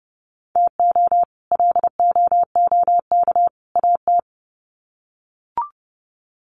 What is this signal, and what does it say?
Morse code, heard as a keyed tone, about 30 wpm, 700 Hz. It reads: TO LOOK AT
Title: Sets of 3 Words 30wpm